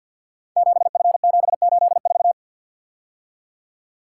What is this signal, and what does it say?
Morse code, heard as a keyed tone, about 52 wpm, 700 Hz. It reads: 7F784
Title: Code Groups - 5 Letter:Numbers - 50wpm